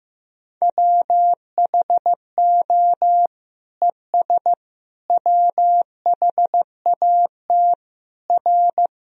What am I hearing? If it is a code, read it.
Morse code, heard as a keyed tone, about 15 wpm, 700 Hz. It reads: WHO ES WHAT R
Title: Sentences from Top 100 Words and Abbreviations 15wpm